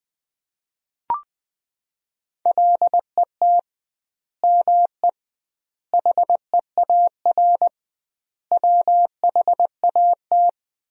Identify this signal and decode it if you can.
Morse code, heard as a keyed tone, about 21 wpm, 700 Hz. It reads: LET ME HEAR WHAT
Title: 2x Word-Spacing - Sentences from Top 300 Words - 20wpm